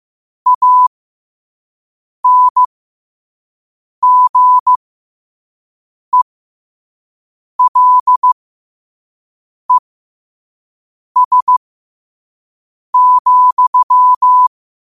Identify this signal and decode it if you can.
Morse code, heard as a keyed tone, about 15 wpm, spaced as for 5 wpm, 1000 Hz. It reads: ANGELES,
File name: qotdfiles / 5wpm